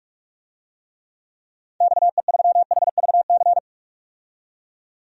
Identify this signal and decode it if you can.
Morse code, heard as a keyed tone, about 46 wpm, 700 Hz. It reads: XE3HVC
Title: DXCC-100 Call Signs - 6 Characters - 45wpm